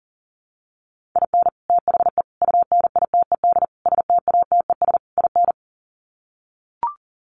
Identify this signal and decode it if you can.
Morse code, heard as a keyed tone, about 40 wpm, 700 Hz. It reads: IN THE UNITED STATES IN
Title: Sets of 5 Words 40wpm